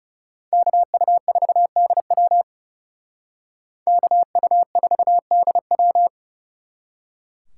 Morse code, heard as a keyed tone, about 37 wpm, 700 Hz. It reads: KU4DW KU4DW
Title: Speed Racing - USA 2x2 Call Signs 30wpm